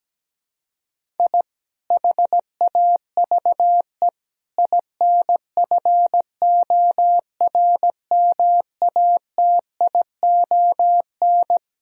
Morse code, heard as a keyed tone, about 17 wpm, 700 Hz. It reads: I HAVE INFORMATION
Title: Sentences from Top 1000 Words - Review Part 3 - 17wpm